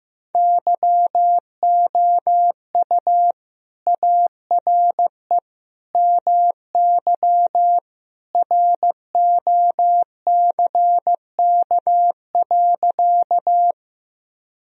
Morse code, heard as a keyed tone, about 15 wpm, 700 Hz. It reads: YOU ARE MY ROCK.
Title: Sentences from Top 1000 Words - Review Part 2 - 15wpm